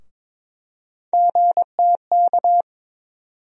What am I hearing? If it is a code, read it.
Morse code, heard as a keyed tone, about 22 wpm, 700 Hz. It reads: GTK
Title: USA Call Sign Postfixes - 3 Letters 22wpm